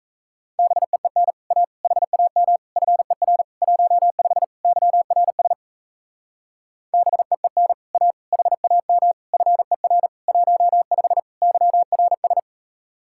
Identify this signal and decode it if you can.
Morse code, heard as a keyed tone, about 37 wpm, 700 Hz. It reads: BEEN A HAM FER 15 YRS BEEN A HAM FER 15 YRS